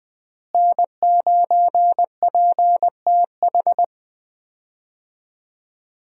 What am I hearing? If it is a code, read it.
Morse code, heard as a keyed tone, about 20 wpm, 700 Hz. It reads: N9PTH
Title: USA Call Signs - 5 Characters - Repeated 2x - 20wpm